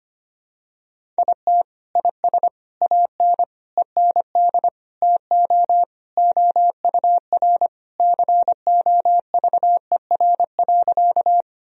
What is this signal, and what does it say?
Morse code, heard as a keyed tone, about 25 wpm, 700 Hz. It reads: IT IS AN END TO OUR COVER.
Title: Sentences from Top 1000 Words - Review Part 1 - 25wpm